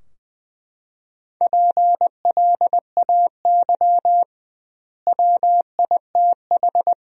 Morse code, heard as a keyed tone, about 20 wpm, 700 Hz. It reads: PLAY WITH
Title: 2x Word-Spacing - Sentences from Top 200 Words - 20wpm